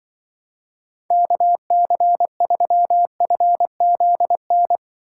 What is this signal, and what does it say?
Morse code, heard as a keyed tone, about 23 wpm, 700 Hz. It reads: KC3FZN
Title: Speed Racing - USA 2x3 Call Signs 20wpm